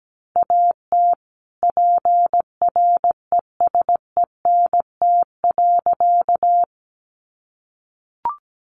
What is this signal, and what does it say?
Morse code, heard as a keyed tone, about 17 wpm, 700 Hz. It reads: AT PRESENT.
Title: Sentences from Top 1200 Words 17wpm